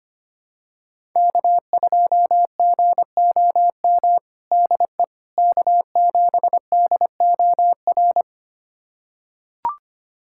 Morse code, heard as a keyed tone, about 25 wpm, 700 Hz. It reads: K2GOM DE K7DOR
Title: USA Callsign DE Callsign - 5 Characters - 25wpm